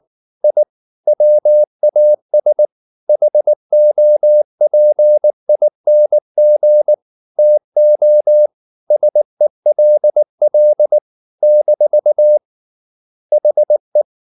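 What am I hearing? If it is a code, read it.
Morse code, heard as a keyed tone, about 19 wpm, 600 Hz. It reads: I WAS HOPING TO SELL - HE
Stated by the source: Morse code at 19 wpm from CW med Gnister, reading Ken Follett's novel Never in English.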